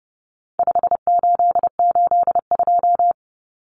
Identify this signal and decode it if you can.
Morse code, heard as a keyed tone, about 30 wpm, 700 Hz. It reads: 5882
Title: Numbers - 4-Digit 30wpm